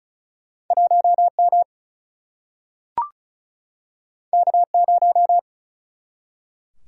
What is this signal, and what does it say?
Morse code, heard as a keyed tone, about 36 wpm, 700 Hz. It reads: 1M K0
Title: Single-Letter Single-Number 35wpm